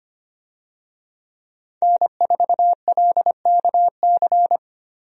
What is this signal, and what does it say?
Morse code, heard as a keyed tone, about 25 wpm, 700 Hz. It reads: N4LKC
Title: USA Call Signs - 5 Characters - Repeated 2x - 25wpm